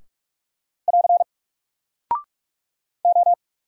Morse code, heard as a keyed tone, about 44 wpm, 700 Hz. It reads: <AR> O
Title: Single Character (Letter:Number:Punctuation:Prosign) 45wpm